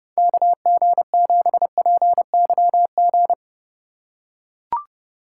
Morse code, heard as a keyed tone, about 30 wpm, 700 Hz. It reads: KG7PYG
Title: USA Call Signs - 3x Character Spacing - 30wpm